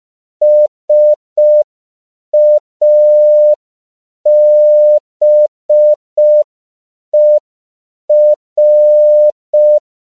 Morse code, heard as a keyed tone, about 5 wpm, 600 Hz. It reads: SABER
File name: 5PPM